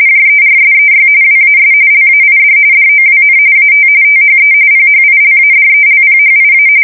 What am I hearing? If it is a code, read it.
RTTY, about 45.45 baud. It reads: JERSEY WAS LAID DOWN AT THE PHILADELPHIA
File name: fsk170 / USS-New-Jersey